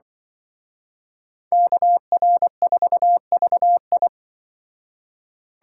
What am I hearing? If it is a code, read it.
Morse code, heard as a keyed tone, about 24 wpm, 700 Hz. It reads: KR4VI
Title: Speed Racing - USA Call Signs 20wpm